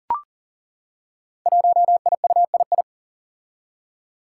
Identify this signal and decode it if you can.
Morse code, heard as a keyed tone, about 42 wpm, 700 Hz. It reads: 1IUII
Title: Five Challenging Letter-Number Combinations 40wpm